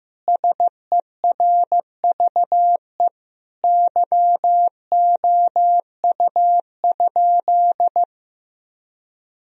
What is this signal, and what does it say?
Morse code, heard as a keyed tone, about 15 wpm, 700 Hz. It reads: SERVE YOU?
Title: Sentences from Top 1000 Words - Review Part 2 - 15wpm